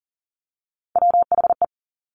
Morse code, heard as a keyed tone, about 38 wpm, 700 Hz. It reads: WHE
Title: Top Three Letter Combinations in Words 40wpm